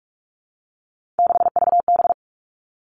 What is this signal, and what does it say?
Morse code, heard as a keyed tone, about 46 wpm, 700 Hz. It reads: 6VB
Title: Three Challenging Letter-Number Combinations 45wpm